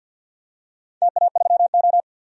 Morse code, heard as a keyed tone, about 48 wpm, 700 Hz. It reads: TA3O
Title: DXCC-100 Call Signs - 4 Characters - 50wpm